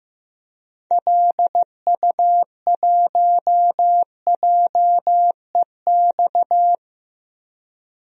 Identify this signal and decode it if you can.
Morse code, heard as a keyed tone, about 15 wpm, 700 Hz. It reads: LU1JEX
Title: DXCC-100 Call Signs - 6 Characters - 15wpm